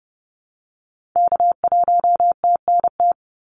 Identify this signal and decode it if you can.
Morse code, heard as a keyed tone, about 30 wpm, 700 Hz. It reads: K1TNT
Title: NA SOTA Activator Callsigns 30wpm